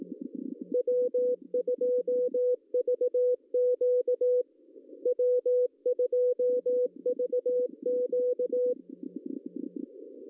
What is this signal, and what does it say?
Morse code, heard as a keyed tone, about 18 wpm, 500 Hz. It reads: W2VQ W2VQ
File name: CW / cqww-cw-k5tr